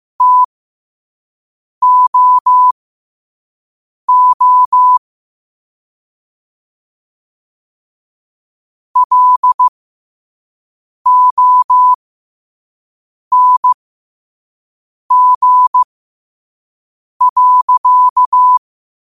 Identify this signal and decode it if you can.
Morse code, heard as a keyed tone, about 15 wpm, spaced as for 5 wpm, 1000 Hz. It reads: TOO LONG.
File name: qotdfiles / 5wpm